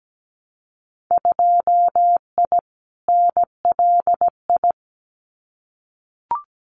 Morse code, heard as a keyed tone, about 17 wpm, 700 Hz. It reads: 2I NLI
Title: Winter Field Day Exchanges 17wpm